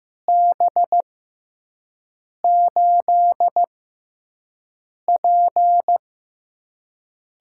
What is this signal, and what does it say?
Morse code, heard as a keyed tone, about 15 wpm, spaced as for 5 wpm, 700 Hz. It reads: B8P
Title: USA Call Signs - 6 Characters - 6x Character Spacing - 15wpm